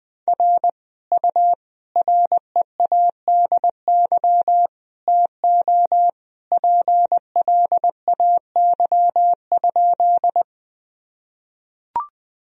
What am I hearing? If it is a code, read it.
Morse code, heard as a keyed tone, about 20 wpm, 700 Hz. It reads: R U READY TO PLAY?
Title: Sentences from Top 400 Words and Abbreviations 20wpm